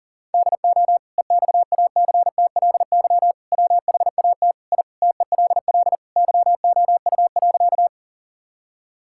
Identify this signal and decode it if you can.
Morse code, heard as a keyed tone, about 40 wpm, 700 Hz. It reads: DO EXACTLY WHAT I TELL YOU.
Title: Sentences from Top 1000 Words - Review Part 6 - 40wpm